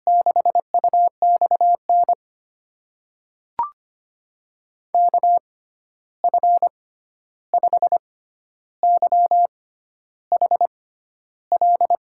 Morse code, heard as a keyed tone, about 25 wpm, spaced as for 11 wpm, 700 Hz. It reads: 6UXN KF5YHL
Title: USA Call Signs - 6 Characters - 6x Character Spacing - 25wpm